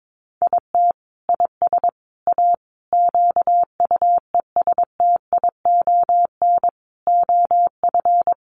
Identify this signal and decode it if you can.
Morse code, heard as a keyed tone, about 22 wpm, 700 Hz. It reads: IT IS A QUESTION OF
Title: Sentences from Top 1100 Words 22wpm